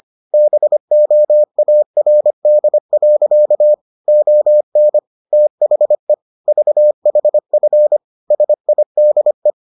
Morse code, heard as a keyed tone, about 25 wpm, 600 Hz. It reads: BOARD. ON THE VHF SIDE